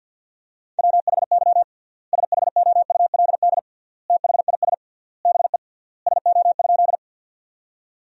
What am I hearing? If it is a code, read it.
Morse code, heard as a keyed tone, about 53 wpm, 700 Hz. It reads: WHY SHOULD THIS BE SO?
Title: 3x Word-Spacing - Sentences from Top 300 Words - 50wpm